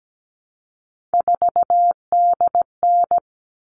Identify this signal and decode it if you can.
Morse code, heard as a keyed tone, about 17 wpm, 700 Hz. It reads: 4DN
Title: All Cut Numbers - 3-Digit 17wpm